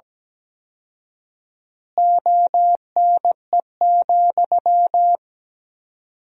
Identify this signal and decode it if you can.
Morse code, heard as a keyed tone, about 17 wpm, 700 Hz. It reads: ONE,
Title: 4x Word-Spacing - Sentences from Top 200 Words - 17wpm